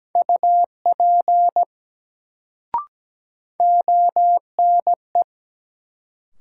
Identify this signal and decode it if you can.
Morse code, heard as a keyed tone, about 17 wpm, 700 Hz. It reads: UP ONE
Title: Sentences from Top 100 Words - New Word Drill - 17wpm